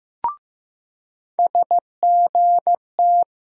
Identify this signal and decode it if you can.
Morse code, heard as a keyed tone, about 15 wpm, 700 Hz. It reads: SGT
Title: USA Call Sign Postfixes - 3 Letters 15wpm